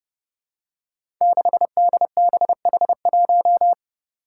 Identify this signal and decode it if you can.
Morse code, heard as a keyed tone, about 30 wpm, 700 Hz. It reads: 6DBH1
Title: Five Challenging Letter-Number Combinations 30wpm